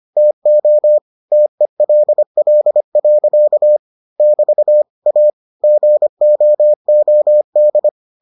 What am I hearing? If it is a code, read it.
Morse code, heard as a keyed tone, about 25 wpm, 600 Hz. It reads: TO TELL. <BT> A GOOD